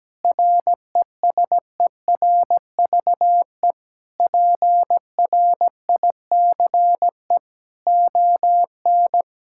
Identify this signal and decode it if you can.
Morse code, heard as a keyed tone, about 17 wpm, 700 Hz. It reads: RESERVE PRICE ON
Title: Sentences from Top 1000 Words - Review Part 6 - 17wpm